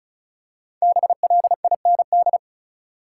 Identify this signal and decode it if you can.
Morse code, heard as a keyed tone, about 36 wpm, 700 Hz. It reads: BLIND
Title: Five Letter Words - 35wpm